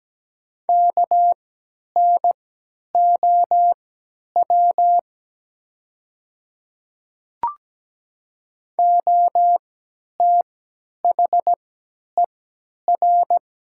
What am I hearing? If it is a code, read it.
Morse code, heard as a keyed tone, about 17 wpm, spaced as for 10 wpm, 700 Hz. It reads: KNOW OTHER
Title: 3x Character Spacing - Top 500 Words - 17wpm